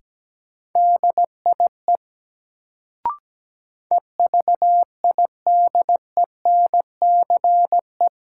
Morse code, heard as a keyed tone, about 17 wpm, 700 Hz. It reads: DIE EVIDENCE
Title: Sentences from Top 1300 Words - New Word Drill - 17wpm